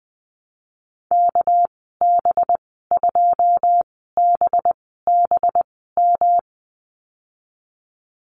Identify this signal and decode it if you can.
Morse code, heard as a keyed tone, about 20 wpm, spaced as for 14 wpm, 700 Hz. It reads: KB2BBM
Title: 219 Lesson 35g - Number 0 - Call Signs 20wpm